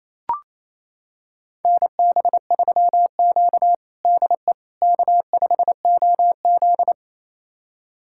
Morse code, heard as a keyed tone, about 28 wpm, 700 Hz. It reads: NB3Q DE K5OZ
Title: USA Callsign DE Callsign - 4 Characters - 28wpm